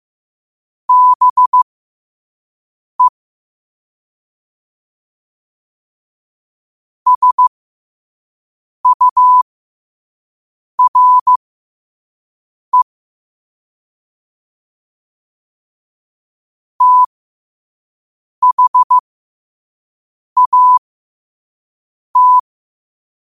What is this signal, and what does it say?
Morse code, heard as a keyed tone, about 15 wpm, spaced as for 5 wpm, 1000 Hz. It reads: BE SURE THAT
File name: qotdfiles / 5wpm